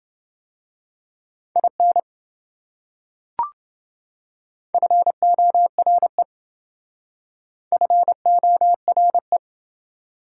Morse code, heard as a keyed tone, about 27 wpm, 700 Hz. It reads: IN FORE FORE